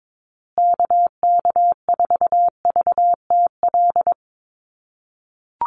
Morse code, heard as a keyed tone, about 22 wpm, 700 Hz. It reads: KK4VTL
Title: USA Call Signs Repeated 2x 22wpm